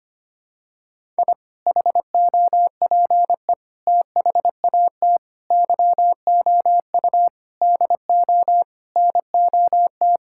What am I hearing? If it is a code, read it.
Morse code, heard as a keyed tone, about 25 wpm, 700 Hz. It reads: I HOPE THAT YOU DO NOT
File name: practice / 25wpm / Sentences with 4-Word Phrases - 25wpm